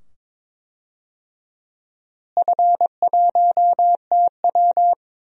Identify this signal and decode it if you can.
Morse code, heard as a keyed tone, about 22 wpm, 700 Hz. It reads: F1TW
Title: DXCC-100 Call Signs - 4 Characters - Repeated 2x - 22wpm